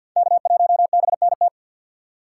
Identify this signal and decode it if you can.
Morse code, heard as a keyed tone, about 48 wpm, 690 Hz. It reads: K1BNT
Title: USA Call Signs - 5 Characters - 6x Character Spacing - 50wpm